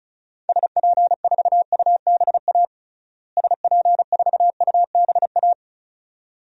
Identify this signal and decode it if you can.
Morse code, heard as a keyed tone, about 36 wpm, 700 Hz. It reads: SP4UBA SP4UBA
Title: DXCC-100 Call Signs - 6 Characters - Repeated 2x - 35wpm